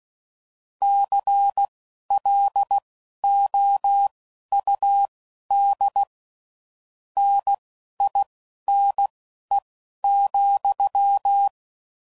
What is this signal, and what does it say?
Morse code, heard as a keyed tone, about 16 wpm, spaced as for 11 wpm, 800 Hz. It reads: CLOUD NINE,
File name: Mixed / Misc / WordPairs